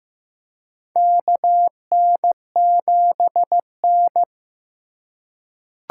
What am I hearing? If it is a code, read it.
Morse code, heard as a keyed tone, about 15 wpm, 700 Hz. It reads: KN7N
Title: USA Call Signs - 4x Character Spacing - 15wpm